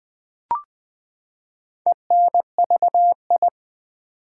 Morse code, heard as a keyed tone, about 20 wpm, 700 Hz. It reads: ENVI
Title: Build A Word - 4 Letters At A Time - 11-Letter Words - 20wpm